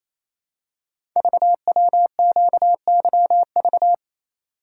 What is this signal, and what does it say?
Morse code, heard as a keyed tone, about 27 wpm, 700 Hz. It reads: VWQYV
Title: Code Groups - 5 Letters - 28wpm